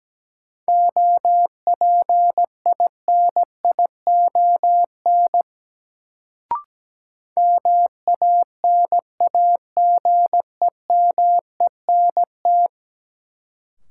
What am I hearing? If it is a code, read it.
Morse code, heard as a keyed tone, about 17 wpm, 700 Hz. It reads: OPINION MANAGEMENT
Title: Sentences from Top 1200 Words - New Word Drill Alt - 17wpm